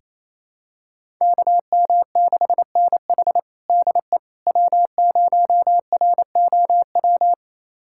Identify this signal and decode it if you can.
Morse code, heard as a keyed tone, about 28 wpm, 700 Hz. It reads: KM6NH DE W0ROW
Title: USA Callsign DE Callsign - 5 Characters - 28wpm